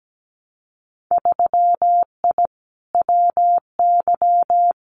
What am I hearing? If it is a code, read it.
Morse code, heard as a keyed tone, about 17 wpm, 700 Hz. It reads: 3I WY
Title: Winter Field Day Exchanges Encore 17wpm